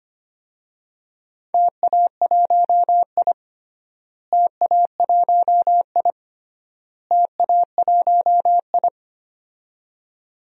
Morse code, heard as a keyed tone, about 25 wpm, 700 Hz. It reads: TA1I TA1I TA1I
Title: DXCC-100 Call Signs - 4 Characters - Repeated 3x - 25wpm